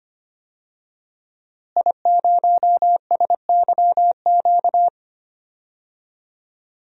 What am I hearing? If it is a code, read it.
Morse code, heard as a keyed tone, about 25 wpm, 700 Hz. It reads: I0SYQ
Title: DXCC-100 Call Signs - 5 Characters - Repeated 3x - 25wpm